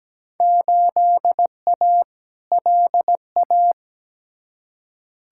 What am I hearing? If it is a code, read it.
Morse code, heard as a keyed tone, about 17 wpm, 700 Hz. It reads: 8A LA
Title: Field Day Exchanges Encore - 17wpm